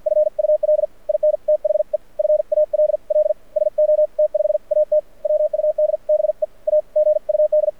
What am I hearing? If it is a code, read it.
Morse code, heard as a keyed tone, about 53 wpm, spaced as for 45 wpm, 600 Hz. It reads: FUL IN THE FALL SO T5AT WUD BE A GUD
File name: archived / CRSnet